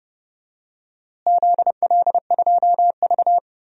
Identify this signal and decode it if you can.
Morse code, heard as a keyed tone, about 31 wpm, 700 Hz. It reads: ZL2V
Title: DXCC-100 Call Signs - 4 Characters - Repeated 2x - 30wpm